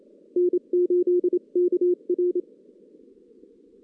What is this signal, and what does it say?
Morse code, heard as a keyed tone, about 27 wpm, 360 Hz. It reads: N8KR
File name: 2016 JAN CW NAQP